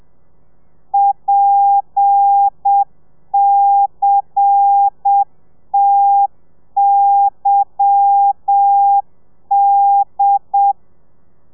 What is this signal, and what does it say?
Morse code, heard as a keyed tone, about 7 wpm, 800 Hz. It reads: PCTYD
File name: telegrafia / i2tfj / Lez10 30